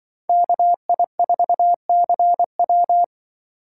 Morse code, heard as a keyed tone, about 25 wpm, 700 Hz. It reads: KI4CW